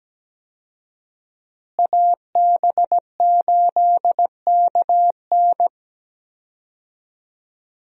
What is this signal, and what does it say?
Morse code, heard as a keyed tone, about 17 wpm, 700 Hz. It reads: AB8KN
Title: USA Call Signs - 5 Characters - Repeated 2x - 17wpm